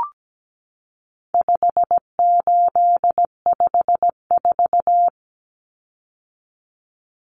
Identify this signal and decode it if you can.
Morse code, heard as a keyed tone, about 17 wpm, 700 Hz. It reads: 5854
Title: Common Cut Numbers - 4-Digit 17wpm